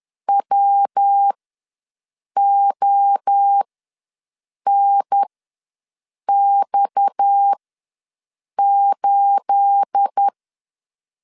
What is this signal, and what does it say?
Morse code, heard as a keyed tone, about 10 wpm, spaced as for 6 wpm, 800 Hz. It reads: WONX8
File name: LEZIONI / 10 SIGNS / 10A SLASH ws 10wpm 4121